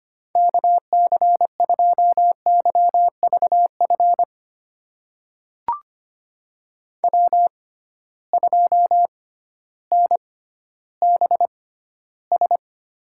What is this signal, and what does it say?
Morse code, heard as a keyed tone, about 25 wpm, spaced as for 14 wpm, 700 Hz. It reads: KC2YVF W2NBS
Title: USA Call Signs - 6x Character Spacing - 25wpm